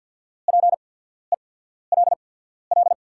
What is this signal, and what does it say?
Morse code, heard as a keyed tone, about 49 wpm, spaced as for 14 wpm, 690 Hz. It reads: PELL
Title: 8x Character Spacing - Top 200 Words - 50wpm